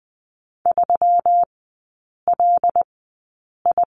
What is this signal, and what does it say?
Morse code, heard as a keyed tone, about 20 wpm, spaced as for 8 wpm, 700 Hz. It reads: 3LI
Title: 121 Lesson 20e - Number 3 - Three Characters 20wpm